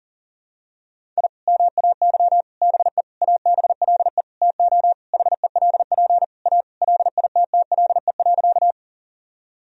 Morse code, heard as a keyed tone, about 40 wpm, 700 Hz. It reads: I MAY BE ABLE TO HELP A LITTLE.
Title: Sentences from Top 1000 Words - Review Part 2 - 40wpm